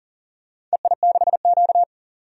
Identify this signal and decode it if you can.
Morse code, heard as a keyed tone, about 40 wpm, 690 Hz. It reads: EI6Q
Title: DXCC-100 Call Signs - 4 Characters - 40wpm